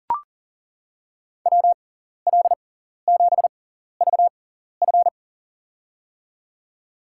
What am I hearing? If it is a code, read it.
Morse code, heard as a keyed tone, about 40 wpm, spaced as for 14 wpm, 700 Hz. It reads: WL7VF
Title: USA Call Signs - 5 Characters - 6x Character Spacing - 40wpm